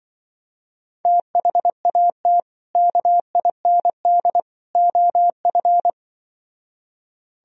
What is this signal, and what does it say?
Morse code, heard as a keyed tone, about 24 wpm, 700 Hz. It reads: THAT KIND OF
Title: Speed-Racing - Sets of 3 Words 20wpm